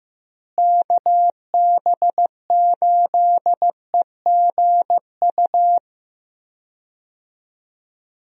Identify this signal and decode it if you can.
Morse code, heard as a keyed tone, about 15 wpm, 700 Hz. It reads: KB8EGU
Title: USA Call Signs - 6 Characters - Repeated 3x - 15wpm